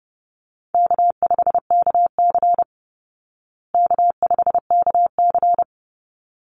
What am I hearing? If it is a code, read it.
Morse code, heard as a keyed tone, about 30 wpm, 700 Hz. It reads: K5KC K5KC
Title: Easy Call Signs Repeated 2x - 30wpm